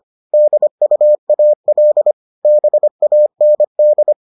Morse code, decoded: DUAL BAND

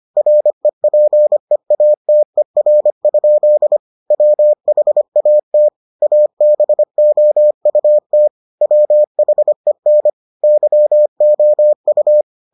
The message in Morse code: REPEATER? WHAT ABOUT WHEN YOU

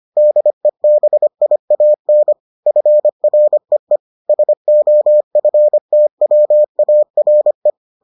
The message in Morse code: DEBIAN FREE SOFTWARE

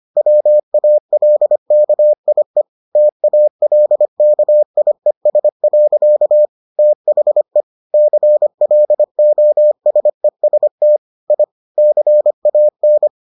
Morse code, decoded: WALKIE TALKIES. THE CLOSEST I CAN

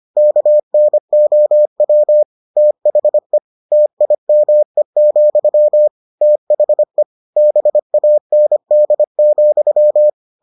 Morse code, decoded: KNOW THE TIME, THE BAND,

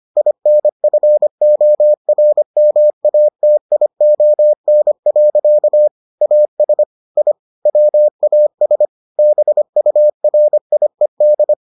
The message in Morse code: INFORMATION. AS I WAS BURIED

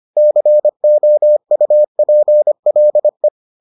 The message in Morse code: COUPLE

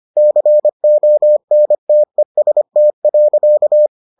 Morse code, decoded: CONTEST.

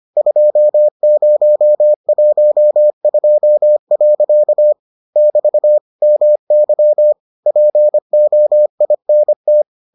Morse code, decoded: 2012. <BT> MY POINT

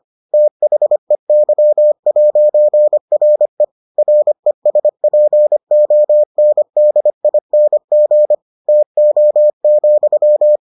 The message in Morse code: THEY'RE RESPONDING TO,

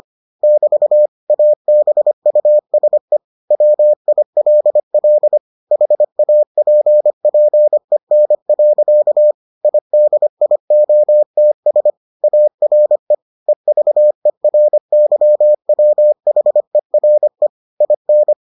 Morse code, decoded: <BT> ABUSE WILL HAPPEN. IDIOTS ARE EVERYWHERE IN

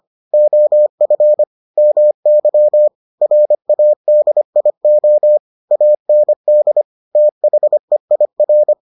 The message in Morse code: OF MY RADIO AND THEIR

